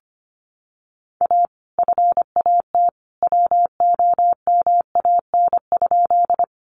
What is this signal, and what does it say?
Morse code, heard as a keyed tone, about 25 wpm, 700 Hz. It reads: A FAT WOMAN?
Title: Sentences from Top 1100 Words 25wpm